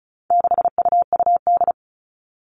Morse code, heard as a keyed tone, about 34 wpm, 700 Hz. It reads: 6UUD
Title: All Cut Numbers - 4-Digit 35wpm